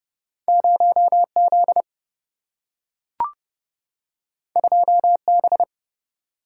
Morse code, heard as a keyed tone, about 30 wpm, 700 Hz. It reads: 0 Z 2 B
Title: Single-Letter Single-Number 30wpm